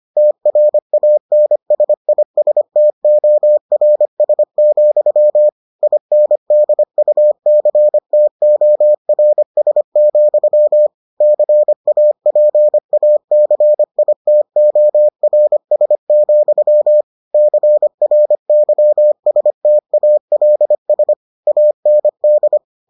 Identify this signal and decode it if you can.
Morse code, heard as a keyed tone, about 25 wpm, 600 Hz. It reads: TRANSISTORS, INDUCTORS, CAPACITORS, CRYSTALS AND